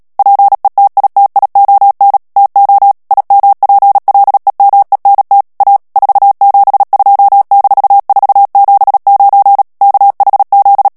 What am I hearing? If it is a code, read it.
Morse code, heard as a keyed tone, about 37 wpm, 800 Hz. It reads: PETITION TO IMPLEMENT A 472-479 KHZ
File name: morse mp3 / ANGLAIS texte clair 3 vitesse 35 mots minute